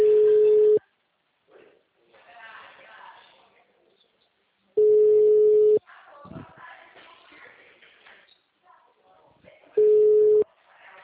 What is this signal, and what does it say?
Ringback tone (Europe, 425 Hz), with speech.
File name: out- -2011- - - .876